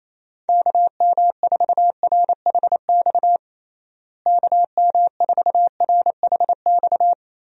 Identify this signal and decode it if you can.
Morse code, heard as a keyed tone, about 28 wpm, 700 Hz. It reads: KM4RHX KM4RHX
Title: USA Call Signs - 6 Characters - Repeated 3x - 28wpm